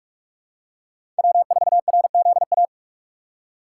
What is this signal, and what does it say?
Morse code, heard as a keyed tone, about 46 wpm, 700 Hz. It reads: W4RZA
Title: USA Call Signs - 5 Characters - 3x Character Spacing - 45wpm